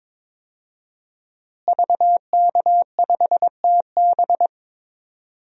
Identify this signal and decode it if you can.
Morse code, heard as a keyed tone, about 22 wpm, 700 Hz. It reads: VK5TB
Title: DXCC-100 Call Signs - 5 Characters - Repeated 3x - 22wpm